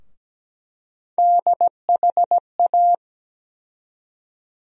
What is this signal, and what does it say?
Morse code, heard as a keyed tone, about 17 wpm, 700 Hz. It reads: DHA
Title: USA Call Sign Postfixes - 3 Letters 17wpm